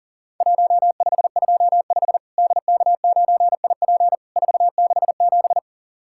Morse code, heard as a keyed tone, about 37 wpm, 700 Hz. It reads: 1525 DK9IP 467
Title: Worked All Europe DX Contest - QTC Exchange - 40wpm